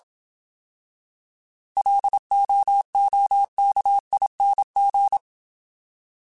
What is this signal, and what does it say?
Morse code, heard as a keyed tone, about 27 wpm, 800 Hz. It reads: LOOKING